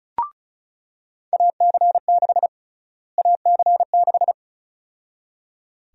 Morse code, heard as a keyed tone, about 34 wpm, 700 Hz. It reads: AC6 AC6
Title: USA Call Sign Prefixes - Repeated 2x - 35wpm